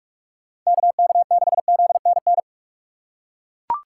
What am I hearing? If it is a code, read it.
Morse code, heard as a keyed tone, about 42 wpm, 700 Hz. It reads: KK6ZNN
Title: USA Call Signs - 6 Characters - 2x Character Spacing - 45wpm